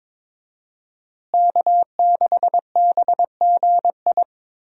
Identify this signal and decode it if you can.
Morse code, heard as a keyed tone, about 22 wpm, 700 Hz. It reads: K6BGI
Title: USA Call Signs - 5 Characters - Repeated 2x - 22wpm